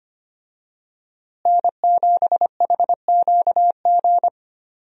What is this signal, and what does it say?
Morse code, heard as a keyed tone, about 25 wpm, 700 Hz. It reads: N7HQG
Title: USA Call Signs - 5 Characters - Repeated 3x - 25wpm